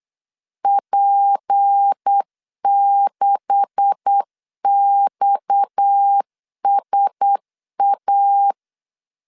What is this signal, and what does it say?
Morse code, heard as a keyed tone, about 8 wpm, 800 Hz. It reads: P6XSA
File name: LEZIONI / 10 SIGNS / 10C DOT 8wpm 3649